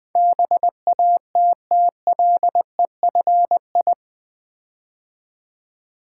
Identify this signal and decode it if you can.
Morse code, heard as a keyed tone, about 20 wpm, 700 Hz. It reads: BATTLEFI
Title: Build A Word - 4 Letters At A Time - 11-Letter Words - 20wpm